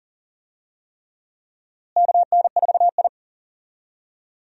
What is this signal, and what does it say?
Morse code, heard as a keyed tone, about 40 wpm, 700 Hz. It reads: KN4I
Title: USA Call Signs - 4 Characters - Repeated 3x - 40wpm